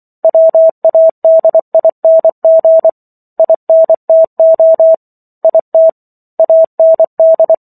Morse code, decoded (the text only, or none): WADING INTO IT AND